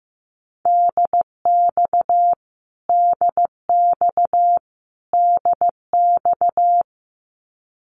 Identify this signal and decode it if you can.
Morse code, heard as a keyed tone, about 15 wpm, 700 Hz. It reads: DX DX DX
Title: Abbreviations Repeated 3x - 15wpm